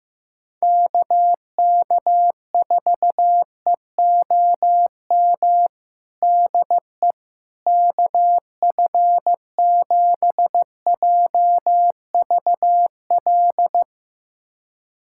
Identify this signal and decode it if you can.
Morse code, heard as a keyed tone, about 15 wpm, 700 Hz. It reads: KK4EOM DE KF7JVL
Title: USA Callsign DE Callsign - 6 Characters - 15wpm